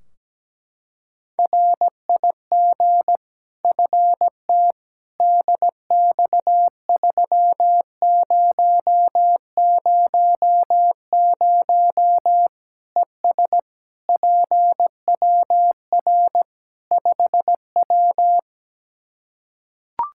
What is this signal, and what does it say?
Morse code, heard as a keyed tone, about 17 wpm, 700 Hz. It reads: RIG FT DX3000 ES PWR 5W
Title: K1USN SST QSO Practice 17wpm